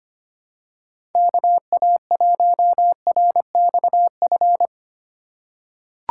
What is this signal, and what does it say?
Morse code, heard as a keyed tone, about 25 wpm, 700 Hz. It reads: KA1RXF
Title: USA Call Signs - 6 Characters - 4x Character Spacing - 25wpm